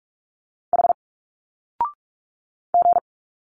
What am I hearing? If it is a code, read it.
Morse code, heard as a keyed tone, about 45 wpm, 700 Hz. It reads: H G
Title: Single Letters 45wpm